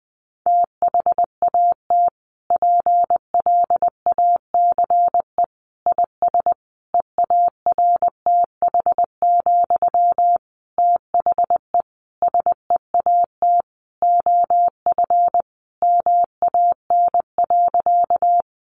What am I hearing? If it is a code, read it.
Morse code, heard as a keyed tone, about 20 wpm, 700 Hz. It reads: THAT PLACE IS EARTH, THE SEAT OF MAN.
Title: Sentences from Top 1500 Words 20wpm